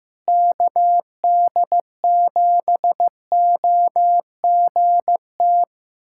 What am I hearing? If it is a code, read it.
Morse code, heard as a keyed tone, about 15 wpm, 700 Hz. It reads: KD7OGT
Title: USA Callsign DE Callsign - 6 Characters - 15wpm